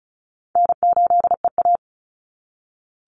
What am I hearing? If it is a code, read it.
Morse code, heard as a keyed tone, about 36 wpm, 700 Hz. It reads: N8EA
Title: Call Signs - 35wpm